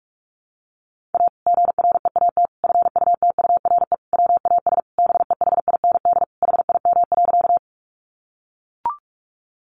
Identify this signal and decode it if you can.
Morse code, heard as a keyed tone, about 44 wpm, 700 Hz. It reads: A GREAT FUTURE WAS BEHIND HIM.
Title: Sentences from Top 700 Words 45wpm